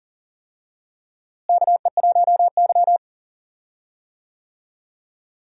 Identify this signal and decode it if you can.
Morse code, heard as a keyed tone, about 40 wpm, 700 Hz. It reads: KE1Y
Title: USA Call Signs - 4 Characters - Repeated 2x - 40wpm